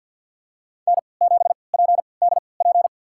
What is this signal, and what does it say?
Morse code, heard as a keyed tone, about 52 wpm, spaced as for 29 wpm, 700 Hz. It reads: N7PDP
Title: USA Call Signs - 3x Character Spacing - 50wpm